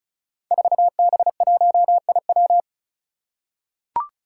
Morse code, heard as a keyed tone, about 36 wpm, 700 Hz. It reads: 4B1IW
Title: Five Challenging Letter-Number Combinations 35wpm